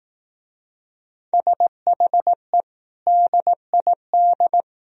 Morse code, heard as a keyed tone, about 18 wpm, 700 Hz. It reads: SHE DID
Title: Speed-Racing - Sets of 5 Words 15wpm